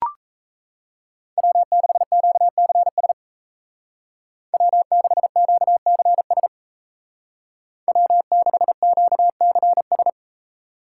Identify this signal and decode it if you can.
Morse code, heard as a keyed tone, about 35 wpm, 700 Hz. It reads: W6QCS W6QCS W6QCS
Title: Speed Racing - USA 1x3 Call Signs 28wpm